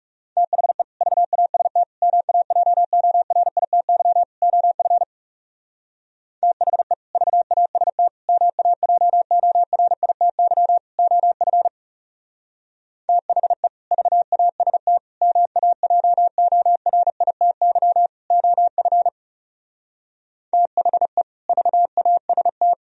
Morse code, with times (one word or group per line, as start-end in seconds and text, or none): 0.4-0.8 THE
1.0-1.8 VAST
2.0-4.2 MAJORITY
4.4-5.0 OF
6.4-6.9 THE
7.2-8.1 VAST
8.3-10.8 MAJORITY
11.0-11.7 OF
13.1-13.7 THE
13.9-15.0 VAST
15.2-18.1 MAJORITY
18.3-19.1 OF
20.5-21.2 THE
21.5-22.7 VAST